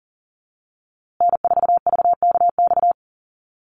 Morse code, heard as a keyed tone, about 40 wpm, 700 Hz. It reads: N4VKX
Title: USA Call Signs Repeated 3x 40wpm